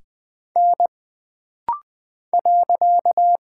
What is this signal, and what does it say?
Morse code, heard as a keyed tone, about 20 wpm, 700 Hz. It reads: N .
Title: Single Character (Letter:Number:Punctuation:Prosign) - Rapid-Fire - 20wpm